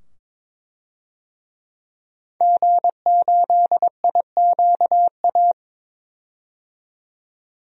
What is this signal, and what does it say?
Morse code, heard as a keyed tone, about 22 wpm, 700 Hz. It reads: G8IQA
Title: DXCC-100 Call Signs - 5 Characters - 22wpm